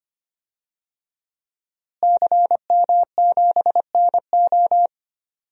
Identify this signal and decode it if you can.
Morse code, heard as a keyed tone, about 25 wpm, 700 Hz. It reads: CM7NO
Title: DXCC-100 Call Signs - 5 Characters - Repeated 3x - 25wpm